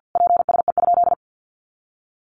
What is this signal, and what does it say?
Morse code, heard as a keyed tone, about 50 wpm, 700 Hz. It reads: PSE?
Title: Common QSO Phrases 50wpm